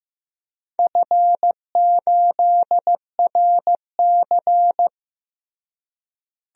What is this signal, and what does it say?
Morse code, heard as a keyed tone, about 15 wpm, 700 Hz. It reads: F8RC
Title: DXCC-100 Call Signs - 4 Characters - Repeated 2x - 15wpm